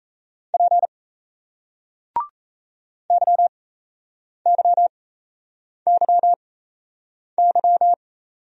Morse code, heard as a keyed tone, about 38 wpm, spaced as for 8 wpm, 700 Hz. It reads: P YYYY